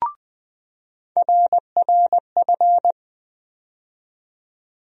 Morse code, heard as a keyed tone, about 20 wpm, 700 Hz. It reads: RRF
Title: USA Call Sign Postfixes - 3 Letters 20wpm